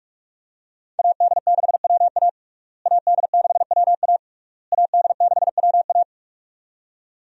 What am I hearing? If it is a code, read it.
Morse code, heard as a keyed tone, about 46 wpm, 700 Hz. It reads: AD6WA AD6WA AD6WA
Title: USA Call Signs - 5 Characters - Repeated 3x - 45wpm